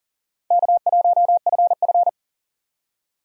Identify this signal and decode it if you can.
Morse code, heard as a keyed tone, about 40 wpm, 700 Hz. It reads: K1FF
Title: USA Call Signs - 4 Characters - 40wpm